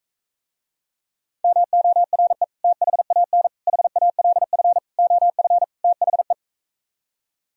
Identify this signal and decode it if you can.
Morse code, heard as a keyed tone, about 45 wpm, 700 Hz. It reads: MORE THAN HALF OF THE